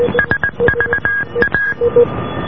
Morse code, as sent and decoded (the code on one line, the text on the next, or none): ... ...- .-
SVA